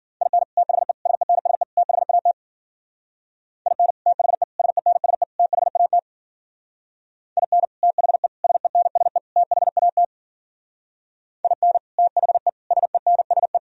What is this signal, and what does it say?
Morse code, heard as a keyed tone, about 49 wpm, 700 Hz. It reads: IN THE SENSE THAT IN THE SENSE THAT IN THE SENSE THAT IN THE SENSE
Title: Speed-Racing - Sets of 4 Words 40wpm